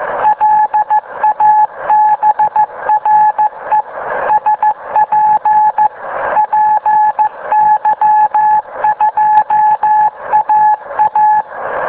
Morse code, heard as a keyed tone, about 14 wpm, 850 Hz. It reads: LABRE SP PY2AA